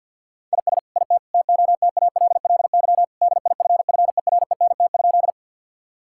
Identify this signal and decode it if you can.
Morse code, heard as a keyed tone, about 50 wpm, 700 Hz. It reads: IS IT TOTALLY DIFFERENT?
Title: Sentences from Top 1600 Words 50wpm